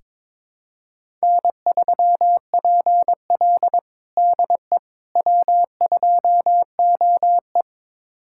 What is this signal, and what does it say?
Morse code, heard as a keyed tone, about 22 wpm, 700 Hz. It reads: N3PL DE W2OE
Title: USA Callsign DE Callsign - 4 Characters - 22wpm